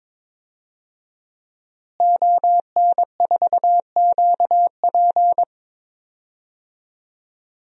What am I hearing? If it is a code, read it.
Morse code, heard as a keyed tone, about 22 wpm, 700 Hz. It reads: ON4QP
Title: DXCC-100 Call Signs - 5 Characters - Repeated 2x - 22wpm